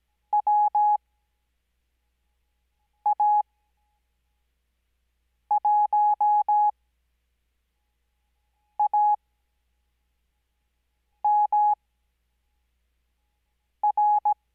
Morse code, heard as a keyed tone, about 17 wpm, spaced as for 4 wpm, 840 Hz. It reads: WA1AMR
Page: Farnsworth Spacing: 15WPM Character speed Spaced at 5WPM.